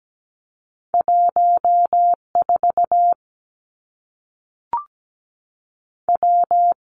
Morse code, heard as a keyed tone, about 17 wpm, 700 Hz. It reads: 1 4 W
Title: Two Challenging Letter-Number Combinations 17wpm